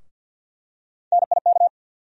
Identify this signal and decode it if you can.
Morse code, heard as a keyed tone, about 49 wpm, 700 Hz. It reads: NIK
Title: USA Call Sign Postfixes - 3 Letters 50wpm